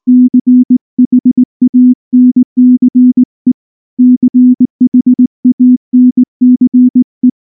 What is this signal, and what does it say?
Morse code, heard as a keyed tone, about 18 wpm, 250 Hz. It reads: CHANCE CHANCE
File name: morse / morse-training-20WPM-6